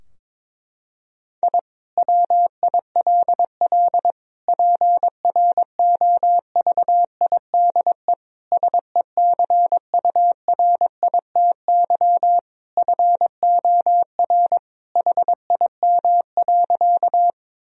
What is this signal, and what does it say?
Morse code, heard as a keyed tone, about 22 wpm, 700 Hz. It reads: I WILL PROVIDE SECURITY FOR HIM.
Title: Sentences from Top 1000 Words - Review Part 4 - 22wpm